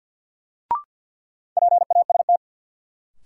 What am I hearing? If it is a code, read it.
Morse code, heard as a keyed tone, about 53 wpm, 700 Hz. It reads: PAST
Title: Sentences from Top 600 Words - New Word Drill - 50wpm